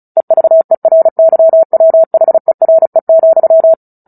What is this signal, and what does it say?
Morse code, heard as a keyed tone, about 35 wpm, 650 Hz. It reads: EVERYWHERE,
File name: Book Two / 09-Chapter 9 - 35 WPM